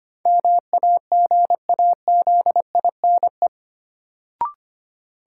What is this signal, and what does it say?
Morse code, heard as a keyed tone, about 25 wpm, 700 Hz. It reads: MAGAZINE
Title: Sentences from Top 1000 Words - Review Part 7 - New Word Drill - 25wpm